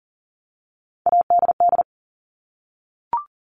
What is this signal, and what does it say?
Morse code, heard as a keyed tone, about 38 wpm, 690 Hz. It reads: ADD
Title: Multi-letter Phonemes 40wpm